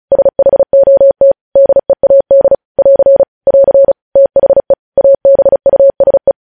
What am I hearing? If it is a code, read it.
Morse code, heard as a keyed tone, about 36 wpm, 550 Hz. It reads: SHOT DEAD <AR> <AR> THE ABUSE
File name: podcast / News headlines-1x-35x35